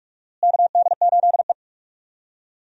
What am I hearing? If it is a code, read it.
Morse code, heard as a keyed tone, about 41 wpm, 700 Hz. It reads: KD8E
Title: USA Call Signs - 4 Characters - Repeated 2x - 45wpm